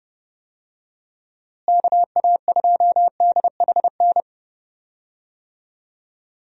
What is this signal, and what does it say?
Morse code, heard as a keyed tone, about 32 wpm, 700 Hz. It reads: KA2DHN
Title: USA Call Signs - 6 Characters - Repeated 2x - 30wpm